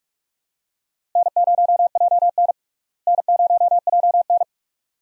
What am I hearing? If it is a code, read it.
Morse code, heard as a keyed tone, about 44 wpm, 700 Hz. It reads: N0JN N0JN
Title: USA Call Signs - 4 Characters - Repeated 2x - 45wpm